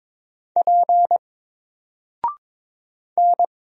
Morse code, heard as a keyed tone, about 22 wpm, 700 Hz. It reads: P N
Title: Single Letter-Number Alt 22wpm